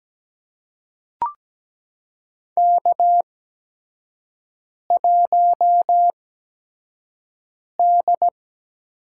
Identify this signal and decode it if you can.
Morse code, heard as a keyed tone, about 17 wpm, spaced as for 5 wpm, 700 Hz. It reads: K1D
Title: USA Call Signs - 4 Characters - 8x Character Spacing - 17wpm